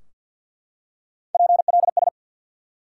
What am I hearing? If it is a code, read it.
Morse code, heard as a keyed tone, about 53 wpm, 700 Hz. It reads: PLS
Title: Top 180 QSO Elements 50wpm